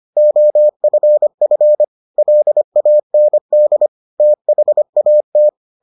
Morse code, heard as a keyed tone, about 25 wpm, 600 Hz. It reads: OFF LAND THAT